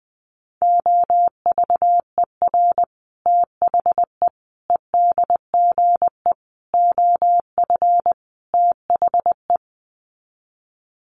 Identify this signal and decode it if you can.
Morse code, heard as a keyed tone, about 20 wpm, 700 Hz. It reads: OVER THE EDGE OF THE
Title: Sets of 5 Words Encore2 20wpm